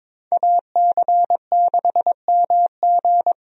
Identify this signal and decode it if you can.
Morse code, heard as a keyed tone, about 21 wpm, 700 Hz. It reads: AC6MG
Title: USA Call Signs - 5 Characters - 22wpm